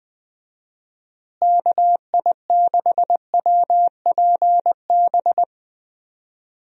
Morse code, heard as a keyed tone, about 20 wpm, 700 Hz. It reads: KI6WPB
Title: USA Call Signs - 6 Characters - Repeated 3x - 20wpm